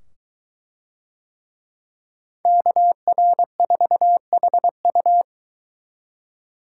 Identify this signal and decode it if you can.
Morse code, heard as a keyed tone, about 22 wpm, 700 Hz. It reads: KR4HU